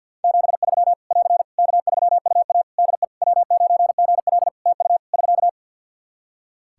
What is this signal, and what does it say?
Morse code, heard as a keyed tone, about 49 wpm, 700 Hz. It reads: 73 <AR> K3UA DE W9GL TU <SK>
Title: Practice QSO - Exchange 4 - Ending - 50wpm